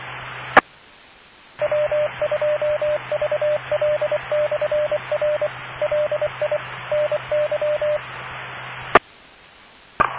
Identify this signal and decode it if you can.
Morse code, heard as a keyed tone, about 23 wpm, 600 Hz, with noise. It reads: E W2VL/R LI NY E E